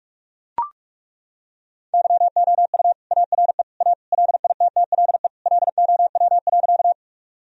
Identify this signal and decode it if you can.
Morse code, heard as a keyed tone, about 42 wpm, 700 Hz. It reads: YOU ARE A LITTLE LOW.
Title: Sentences from Top 1000 Words - Review Part 1 - 45wpm